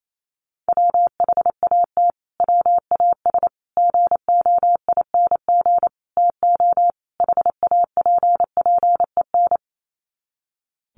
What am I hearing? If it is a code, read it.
Morse code, heard as a keyed tone, about 28 wpm, 700 Hz. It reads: WHAT WAS GOING TO HAPPEN
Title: Sets of 5 Words Encore 28wpm